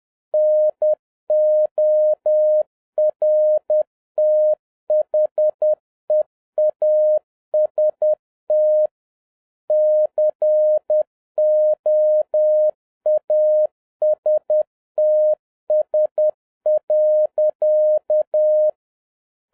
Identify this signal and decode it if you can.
Morse code, heard as a keyed tone, about 10 wpm, 610 Hz. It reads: NORTHEAST COASTS.
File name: 2011-07-mp3 / n31 THE 3.11 2 10wpm